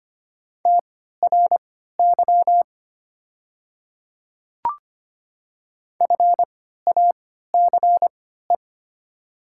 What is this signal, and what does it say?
Morse code, heard as a keyed tone, about 25 wpm, spaced as for 14 wpm, 700 Hz. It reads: TRY FACE
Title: 3x Character Spacing - Top 300 Words - 25wpm